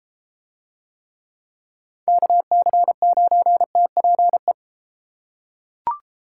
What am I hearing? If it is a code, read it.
Morse code, heard as a keyed tone, about 32 wpm, 700 Hz. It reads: KC9TPE